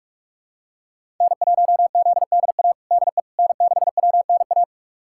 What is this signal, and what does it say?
Morse code, heard as a keyed tone, about 46 wpm, 700 Hz. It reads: N1ZDA DE N6WNA
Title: USA Callsign DE Callsign - 5 Characters - 45wpm